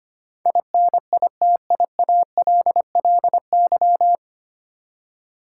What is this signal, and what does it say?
Morse code, heard as a keyed tone, about 25 wpm, 700 Hz. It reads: INITIALLY
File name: Nine Letter Words - 25wpm